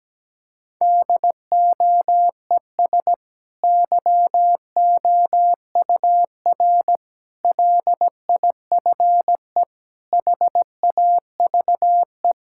Morse code, heard as a keyed tone, about 17 wpm, 700 Hz. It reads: DOES YOUR LIFE HAVE
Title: Sentences from Top 1000 Words - Review Part 6 - 17wpm